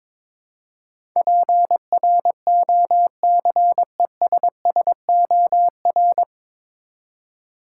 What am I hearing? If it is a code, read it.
Morse code, heard as a keyed tone, about 22 wpm, 700 Hz. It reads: PROCESSOR
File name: 9 Letter Words - First 3 Letters Given - 22wpm